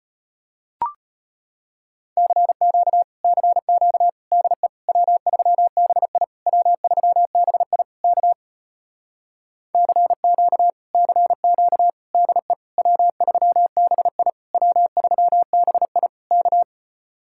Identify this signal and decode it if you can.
Morse code, heard as a keyed tone, about 33 wpm, 700 Hz. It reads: CQ CQ DE W3BI W3BI K CQ CQ DE W3BI W3BI K